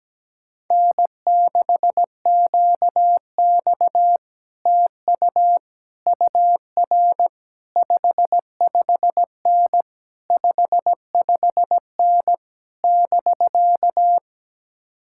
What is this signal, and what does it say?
Morse code, heard as a keyed tone, about 17 wpm, 700 Hz. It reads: N6QX TU UR 55N 55N <BK>
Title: Speed-Racing Common QSO Phrases - 17wpm